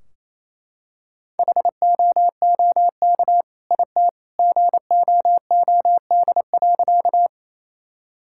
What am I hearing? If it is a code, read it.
Morse code, heard as a keyed tone, about 28 wpm, 700 Hz. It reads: HOOK IT GOOD.
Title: Sentences from Top 1600 Words 28wpm